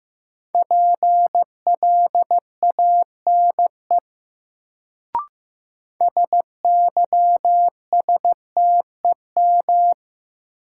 Morse code, heard as a keyed tone, about 15 wpm, 700 Hz. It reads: PLANE SYSTEM
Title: Sentences from Top 1000 Words - Review Part 3 - New Word Drill - 15wpm